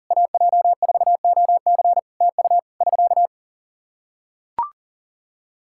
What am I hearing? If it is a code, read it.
Morse code, heard as a keyed tone, about 38 wpm, 700 Hz. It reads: AJ4OC TU <SK>
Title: Practice QSO - Exchange 4 - Ending - 40wpm